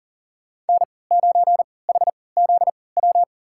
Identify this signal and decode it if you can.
Morse code, heard as a keyed tone, about 39 wpm, spaced as for 22 wpm, 700 Hz. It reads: N9HZW
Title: USA Call Signs - 5 Characters - 3x Character Spacing - 40wpm